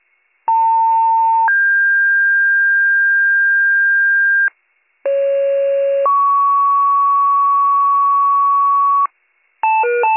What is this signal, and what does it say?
Two-tone page, heard as a one-second tone, then a three-second tone, about 900 then 1600 Hz, about 550 then 1120 Hz, with noise. It is heard from 0.5 to 9.1 s.